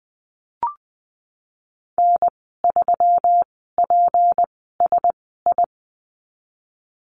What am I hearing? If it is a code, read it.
Morse code, heard as a keyed tone, about 19 wpm, spaced as for 14 wpm, 700 Hz. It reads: N3PSI
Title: 207 Lesson 33g - Letter j - Call Signs 20wpm